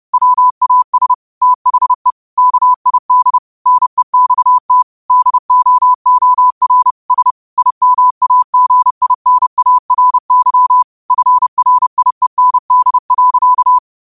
Morse code, heard as a keyed tone, about 30 wpm, 1000 Hz. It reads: WAS THE KID NEXT DOOR S IMAGINARY FRIEND.